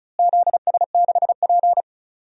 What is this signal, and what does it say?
Morse code, heard as a keyed tone, about 34 wpm, 700 Hz. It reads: ZS6P
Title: DXCC-100 Call Signs - 4 Characters - Repeated 3x - 35wpm